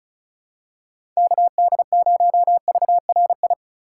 Morse code, heard as a keyed tone, about 36 wpm, 700 Hz. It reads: KD0VRI
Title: USA Call Signs - 6 Characters - 2x Character Spacing - 35wpm